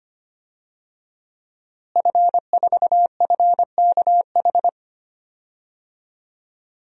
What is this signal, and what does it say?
Morse code, heard as a keyed tone, about 25 wpm, 700 Hz. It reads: F4FKH
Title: DXCC-100 Call Signs - 5 Characters - Repeated 3x - 25wpm